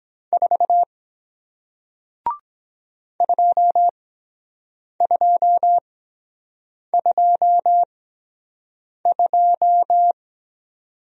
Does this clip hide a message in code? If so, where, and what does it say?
Yes: Morse code from 0.3 to 10.1 s, heard as a keyed tone, about 23 wpm, spaced as for 7 wpm, 700 Hz. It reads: 4 2222